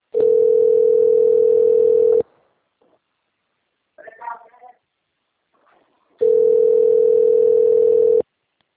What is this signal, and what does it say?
Ringback tone (North America), 440 + 480 Hz, with speech.
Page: Dantas - PE